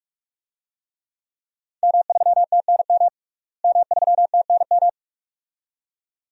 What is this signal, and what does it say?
Morse code, heard as a keyed tone, about 44 wpm, 700 Hz. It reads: M3TNM M3TNM
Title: DXCC-100 Call Signs - 5 Characters - Repeated 2x - 45wpm